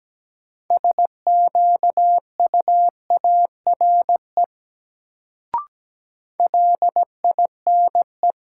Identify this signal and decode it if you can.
Morse code, heard as a keyed tone, about 17 wpm, 700 Hz. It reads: SQUARE LINE
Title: Sentences from Top 1200 Words - New Word Drill Alt - 17wpm